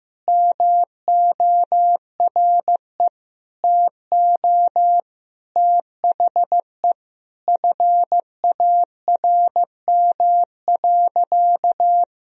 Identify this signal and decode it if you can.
Morse code, heard as a keyed tone, about 16 wpm, 700 Hz. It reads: MORE TO THE FARM.
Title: Sentences from Top 1000 Words - Review Part 2 - 15wpm